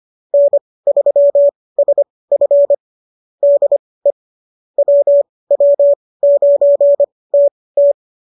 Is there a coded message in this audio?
Yes: Morse code from 0.3 to 7.9 s, heard as a keyed tone, about 25 wpm, spaced as for 18 wpm, 570 Hz. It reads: N3SF DE WW9TT